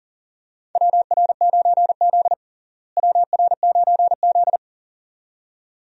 Morse code, heard as a keyed tone, about 40 wpm, 700 Hz. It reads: WR9Z WR9Z
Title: USA Call Signs - 4 Characters - Repeated 2x - 40wpm